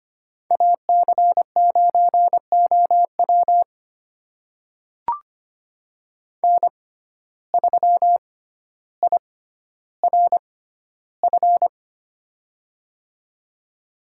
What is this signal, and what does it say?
Morse code, heard as a keyed tone, about 25 wpm, spaced as for 13 wpm, 700 Hz. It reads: AC9OW N3IRF
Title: USA Call Signs - 6x Character Spacing - 25wpm